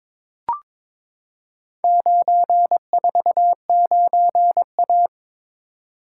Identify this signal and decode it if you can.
Morse code, heard as a keyed tone, about 22 wpm, 700 Hz. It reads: 949A
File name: practice / 22wpm / Four Letter-Number Combinations - 22wpm